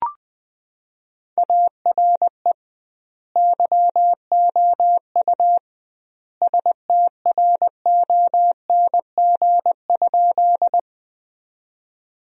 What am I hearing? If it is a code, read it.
Morse code, heard as a keyed tone, about 21 wpm, 700 Hz. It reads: ARE YOU STRONG?
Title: 2x Word-Spacing - Sentences from Top 500 Words - 20wpm